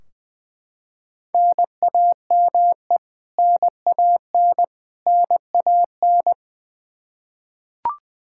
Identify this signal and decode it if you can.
Morse code, heard as a keyed tone, about 20 wpm, 700 Hz. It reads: NAME NAN NAN
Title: K1USN SST QSO Practice 20wpm